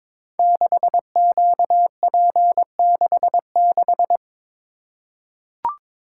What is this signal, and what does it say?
Morse code, heard as a keyed tone, about 22 wpm, 700 Hz. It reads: 6QP66
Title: Code Groups - 5 Letter:Numbers - 22wpm